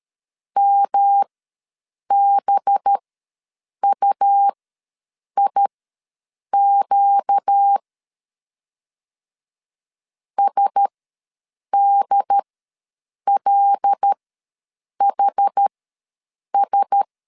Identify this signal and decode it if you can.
Morse code, heard as a keyed tone, about 13 wpm, spaced as for 7 wpm, 800 Hz. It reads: MBUIQ SDLHS
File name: 5 LB ws 12wpm